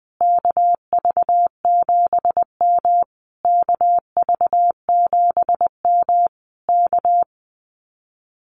Morse code, heard as a keyed tone, about 21 wpm, 700 Hz. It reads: KV7M KV7M K
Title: Common QSO Phrases Encore 20wpm